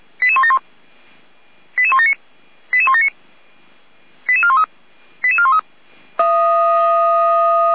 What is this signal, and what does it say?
Five-tone selective call (ZVEI), heard as quick runs of five tones, tones of about 70 ms. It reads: 88171 88179 88179 88413 88413